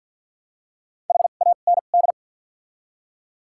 Morse code, heard as a keyed tone, about 50 wpm, spaced as for 36 wpm, 700 Hz. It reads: HAND
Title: 2x Character Spacing - Top 200 Words - 50wpm